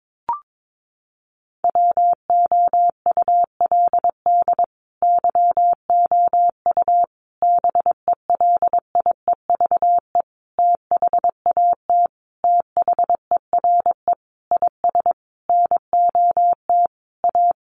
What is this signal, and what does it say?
Morse code, heard as a keyed tone, about 22 wpm, 700 Hz. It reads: WOULD YOU BELIEVE THAT THERE IS NOT A
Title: Sentences from Top 1300 Words 22wpm